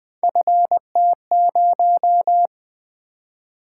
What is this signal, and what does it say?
Morse code, heard as a keyed tone, about 20 wpm, 700 Hz. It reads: FT0
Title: Three Letter-Number Combinations - 20wpm